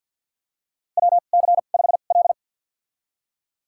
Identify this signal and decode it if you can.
Morse code, heard as a keyed tone, about 52 wpm, spaced as for 37 wpm, 700 Hz. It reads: WC5L
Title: USA Call Signs - 4 Characters - 2x Character Spacing - 50wpm